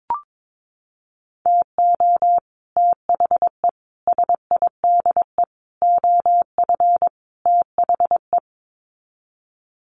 Morse code, heard as a keyed tone, about 22 wpm, 700 Hz. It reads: TO THE SIDE OF THE
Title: Sets of 5 Words 22wpm